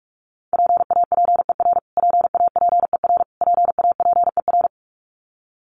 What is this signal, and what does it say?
Morse code, heard as a keyed tone, about 44 wpm, 700 Hz. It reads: PAPER PAPER PAPER
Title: Top 500 Words Repeated 45wpm